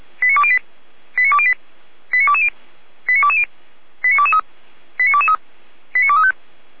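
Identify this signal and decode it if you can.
Five-tone selective call (ZVEI), heard as quick runs of five tones, tones of about 70 ms. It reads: 89208 89208 89220 89220 89223 89223 89235
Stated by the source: ZVEI five-tone selective call received on an SDR.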